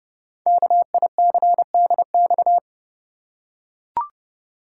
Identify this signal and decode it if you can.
Morse code, heard as a keyed tone, about 30 wpm, 700 Hz. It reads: KICDX
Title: Code Groups - 5 Letter:Numbers - 30wpm